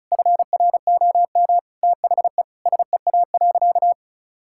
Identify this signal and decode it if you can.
Morse code, heard as a keyed tone, about 36 wpm, 700 Hz. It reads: FROM THE SEA.
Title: Sentences from Top 1000 Words - Review Part 6 - 35wpm